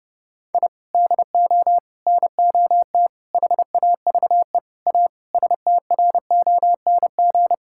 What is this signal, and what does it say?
Morse code, heard as a keyed tone, about 32 wpm, 700 Hz. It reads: I DO NOT HAVE A STRONG
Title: Sentences from Top 1000 Words - Review Part 5 - 30wpm